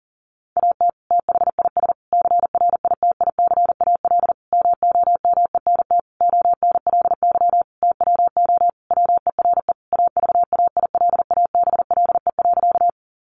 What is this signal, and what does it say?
Morse code, heard as a keyed tone, about 40 wpm, 700 Hz. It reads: AT THIS CRITICAL MOMENT ONLY TWO WERE AVAILABLE.
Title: Sentences from Top 1100 Words 40wpm